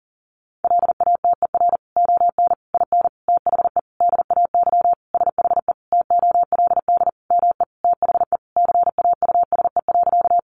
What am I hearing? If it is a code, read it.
Morse code, heard as a keyed tone, about 40 wpm, 700 Hz. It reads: LATER ON IN THE DAY SHE TOLD ME THE CAUSE.
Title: Sentences from Top 700 Words 40wpm